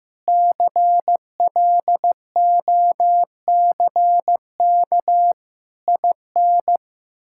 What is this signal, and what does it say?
Morse code, heard as a keyed tone, about 15 wpm, 700 Hz. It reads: CLOCK IN
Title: Pangram Challenge v2 - 15wpm